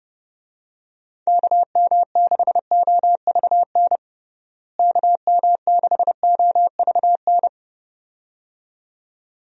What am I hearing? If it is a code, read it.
Morse code, heard as a keyed tone, about 30 wpm, 700 Hz. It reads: KM6OVN KM6OVN
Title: USA Call Signs - 6 Characters - Repeated 2x - 30wpm